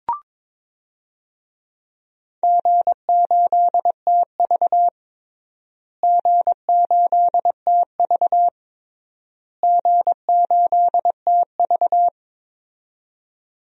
Morse code, heard as a keyed tone, about 22 wpm, 700 Hz. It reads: G8TV G8TV G8TV
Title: DXCC-100 Call Signs - 4 Characters - Repeated 3x - 22wpm